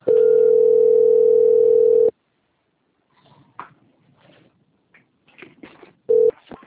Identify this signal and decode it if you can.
Ringback tone (North America), 440 + 480 Hz.